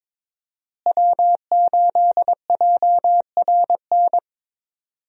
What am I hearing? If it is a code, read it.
Morse code, heard as a keyed tone, about 21 wpm, 700 Hz. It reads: W8JRN
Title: USA Call Signs - 5 Characters - Repeated 3x - 22wpm